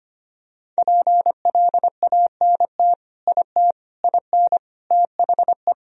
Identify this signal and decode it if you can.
Morse code, heard as a keyed tone, about 25 wpm, 700 Hz. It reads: PLANT IT IN THE
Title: Sentences from Top 1000 Words - Review Part 2 - 25wpm